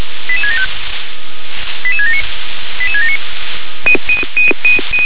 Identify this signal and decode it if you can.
Five-tone selective call (ZVEI), heard as quick runs of five tones, tones of about 80 ms. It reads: 88675 88679 88679